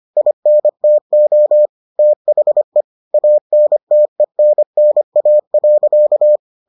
Morse code, heard as a keyed tone, about 25 wpm, 600 Hz. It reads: INTO THE ANTENNA.